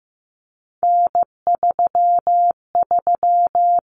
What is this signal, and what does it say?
Morse code, heard as a keyed tone, about 15 wpm, 700 Hz. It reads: N33
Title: Common Cut Numbers - 3-Digit 15wpm